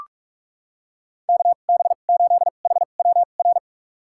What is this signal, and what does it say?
Morse code, heard as a keyed tone, about 46 wpm, spaced as for 32 wpm, 700 Hz. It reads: KB8HWR
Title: USA Call Signs - 6 Characters - 2x Character Spacing - 45wpm